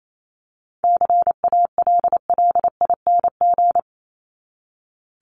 Morse code, heard as a keyed tone, about 28 wpm, 700 Hz. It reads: CALLING
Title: ING Suffix - Top 100 Words with Base Words 28wpm